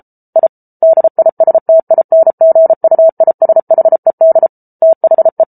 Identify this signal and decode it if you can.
Morse code, heard as a keyed tone, about 34 wpm, 650 Hz. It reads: I DISTINGUISHED THE